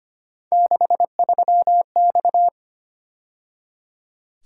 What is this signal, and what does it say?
Morse code, heard as a keyed tone, about 29 wpm, 700 Hz. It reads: 63X
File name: Speed-Racing - Three Letter-Number Combinations - 25wpm